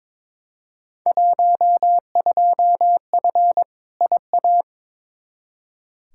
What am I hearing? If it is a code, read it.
Morse code, heard as a keyed tone, about 22 wpm, 700 Hz. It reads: 12F IA
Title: Field Day Exchanges Encore - 22wpm